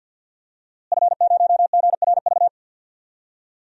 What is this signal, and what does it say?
Morse code, heard as a keyed tone, about 50 wpm, 700 Hz. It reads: F0GRV
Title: DXCC-100 Call Signs - 5 Characters - Repeated 2x - 50wpm